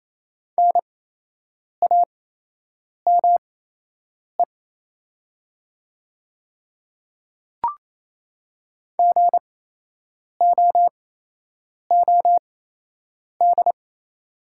Morse code, heard as a keyed tone, about 28 wpm, spaced as for 8 wpm, 700 Hz. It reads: NAME GOOD
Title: 8x Character Spacing - Top 200 Words - 28wpm